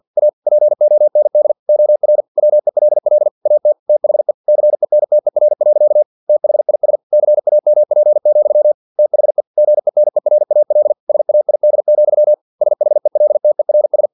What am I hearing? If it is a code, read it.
Morse code, heard as a keyed tone, about 50 wpm, 600 Hz. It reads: A POND OR WELL AT THE CENTER. THIS CAMP, THE GENERAL SAID, SHELTERS
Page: Never 49wpm